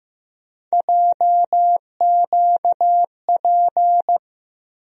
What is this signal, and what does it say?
Morse code, heard as a keyed tone, about 15 wpm, 700 Hz. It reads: JQP
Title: USA Call Sign Postfixes - 3 Letters 15wpm